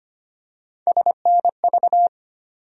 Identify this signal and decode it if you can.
Morse code, heard as a keyed tone, about 25 wpm, 700 Hz. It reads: SNV
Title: USA Call Sign Postfixes - 3 Letters 25wpm